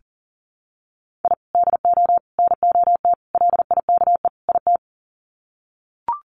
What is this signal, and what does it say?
Morse code, heard as a keyed tone, about 40 wpm, 700 Hz. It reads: I DO NOT LIKE IT
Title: Sets of 5 Words Encore 40wpm